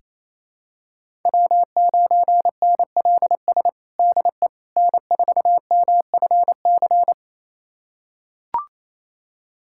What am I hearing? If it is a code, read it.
Morse code, heard as a keyed tone, about 28 wpm, 700 Hz. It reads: W9NLS DE N4MFC
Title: USA Callsign DE Callsign 28wpm